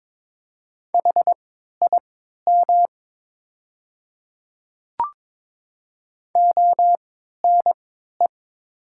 Morse code, heard as a keyed tone, about 22 wpm, spaced as for 13 wpm, 700 Hz. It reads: HIM ONE
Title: 3x Character Spacing - Top 300 Words - 22wpm